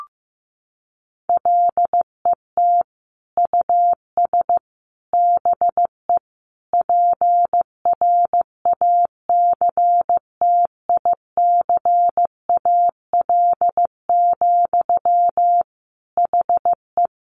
Morse code, heard as a keyed tone, about 15 wpm, 700 Hz. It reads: LET US BE PRACTICAL, HE
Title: Sentences from Top 1400 Words 15wpm